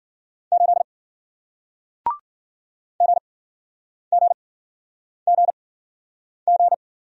Morse code, heard as a keyed tone, about 53 wpm, spaced as for 9 wpm, 700 Hz. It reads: 8 GGGG